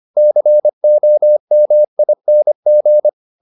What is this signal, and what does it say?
Morse code, heard as a keyed tone, about 25 wpm, 600 Hz. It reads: COMING